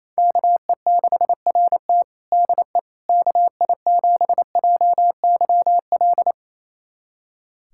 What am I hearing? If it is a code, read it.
Morse code, heard as a keyed tone, about 28 wpm, 700 Hz. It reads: KE6RT DE KI7JYL
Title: USA Callsign DE Callsign 28wpm